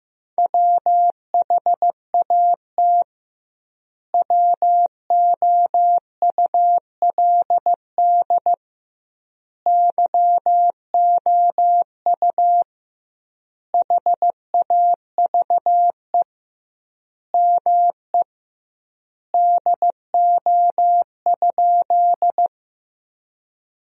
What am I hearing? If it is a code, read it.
Morse code, heard as a keyed tone, about 15 wpm, 700 Hz. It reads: WHAT WOULD YOU HAVE ME DO?
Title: 2x Word-Spacing - Sentences from Top 200 Words - 15wpm